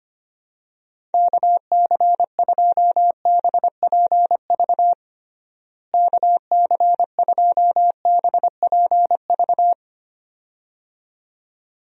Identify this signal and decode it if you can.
Morse code, heard as a keyed tone, about 25 wpm, 700 Hz. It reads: KC2BPV KC2BPV
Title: USA Call Signs - 6 Characters - Repeated 2x - 25wpm